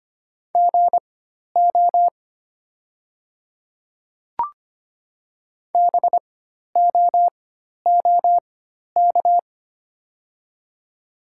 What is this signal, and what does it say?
Morse code, heard as a keyed tone, about 25 wpm, spaced as for 12 wpm, 690 Hz. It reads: GO BOOK
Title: 4x Character Spacing - Top 300 Words - 25wpm